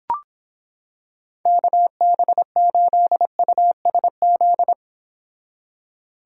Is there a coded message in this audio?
Yes: Morse code from 1.5 to 4.7 s, heard as a keyed tone, about 26 wpm, 700 Hz. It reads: KB8USZ